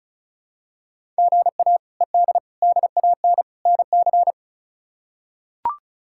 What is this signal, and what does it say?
Morse code, heard as a keyed tone, about 34 wpm, 700 Hz. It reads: GA ED DAN NC
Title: K1USN SST QSO Practice - 35wpm